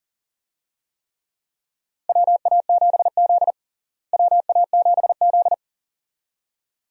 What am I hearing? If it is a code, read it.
Morse code, heard as a keyed tone, about 40 wpm, 700 Hz. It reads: WA7Z WA7Z
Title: USA Call Signs - 4 Characters - Repeated 2x - 40wpm